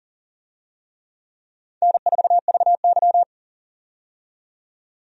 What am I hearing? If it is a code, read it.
Morse code, heard as a keyed tone, about 40 wpm, 700 Hz. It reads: N4VY
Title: USA Call Signs - 4 Characters - 40wpm